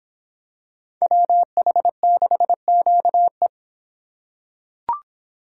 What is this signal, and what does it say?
Morse code, heard as a keyed tone, about 24 wpm, 700 Hz. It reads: WH6QE